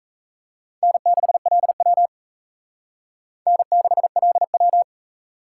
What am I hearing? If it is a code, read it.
Morse code, heard as a keyed tone, about 34 wpm, 700 Hz. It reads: N6LW N6LW
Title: Speed Racing - USA 1x2 Call Signs 28wpm